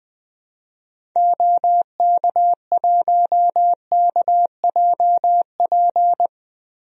Morse code, heard as a keyed tone, about 20 wpm, 700 Hz. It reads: OK1KJP
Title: DXCC-100 Call Signs - 6 Characters - 20wpm